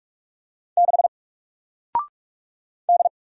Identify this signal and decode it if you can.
Morse code, heard as a keyed tone, about 44 wpm, 700 Hz. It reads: 6 D
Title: Single Character (Letter:Number:Punctuation:Prosign) Alt 45wpm